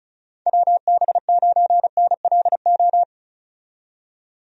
Morse code, heard as a keyed tone, about 36 wpm, 700 Hz. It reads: WB9NLO
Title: USA Call Signs - 6 Characters - 35wpm